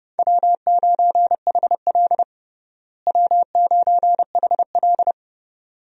Morse code, heard as a keyed tone, about 30 wpm, 700 Hz. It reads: W9HL W9HL
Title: USA Call Signs - 4 Characters - Repeated 2x - 30wpm